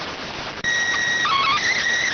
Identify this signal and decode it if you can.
SSTV (slow-scan television), PD 290.